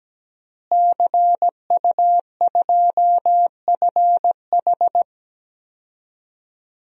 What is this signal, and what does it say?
Morse code, heard as a keyed tone, about 17 wpm, 700 Hz. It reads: CU2FH
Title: DXCC-100 Call Signs - 5 Characters - Repeated 3x - 17wpm